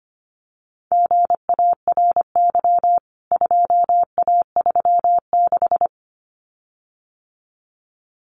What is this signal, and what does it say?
Morse code, heard as a keyed tone, about 25 wpm, 700 Hz. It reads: GARY 2A36
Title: CWT Exchange 25wpm